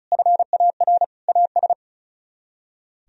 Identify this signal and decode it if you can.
Morse code, heard as a keyed tone, about 44 wpm, 700 Hz. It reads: FAR AS
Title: Speed-Racing - Sets of 3 Words 35wpm